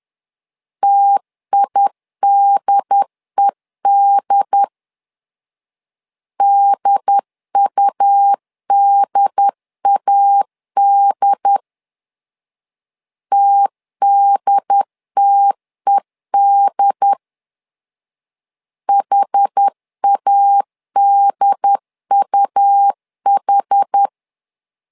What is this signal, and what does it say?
Morse code, heard as a keyed tone, about 10 wpm, 800 Hz. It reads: TIDED DUDAD TDTED HADUH
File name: LEZIONI / 2 UHDN / 2 UHD 10wpm 1826